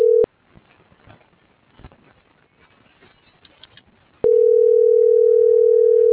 Ringback tone (North America), 440 + 480 Hz.